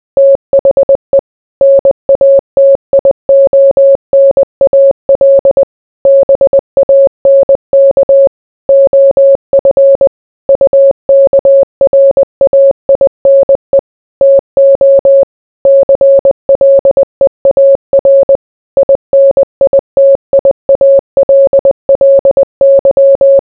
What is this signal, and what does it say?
Morse code, heard as a keyed tone, about 20 wpm, 550 Hz. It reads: THE NATIONAL BANK OF UKRAINE TO CLEAR INITIALLY